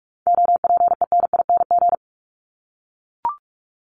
Morse code, heard as a keyed tone, about 44 wpm, 700 Hz. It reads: OPENING
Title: Top 1751-2000 Words 45wpm